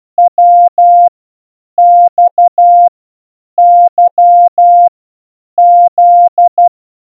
Morse code, heard as a keyed tone, about 12 wpm, 700 Hz. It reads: W X Y Z
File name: A through Z in Morse code